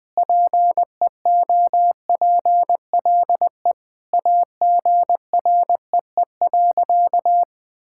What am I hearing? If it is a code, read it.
Morse code, heard as a keyed tone, about 20 wpm, 700 Hz. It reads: PEOPLE AGREE.
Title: Sentences with 4-Word Phrases - 20wpm